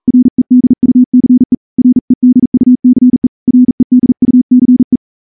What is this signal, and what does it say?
Morse code, heard as a keyed tone, about 38 wpm, 250 Hz. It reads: REDUCE REDUCE REDUCE
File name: morse-training-40WPM